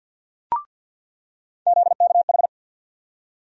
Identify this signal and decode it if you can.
Morse code, heard as a keyed tone, about 48 wpm, 700 Hz. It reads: ZKH
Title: USA Call Sign Postfixes - 3 Letters 50wpm